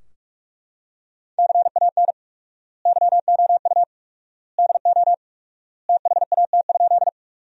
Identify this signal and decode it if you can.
Morse code, heard as a keyed tone, about 42 wpm, 700 Hz. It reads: CAN YOU DO THAT?
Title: 4x Word-Spacing - Sentences from Top 100 Words - 45wpm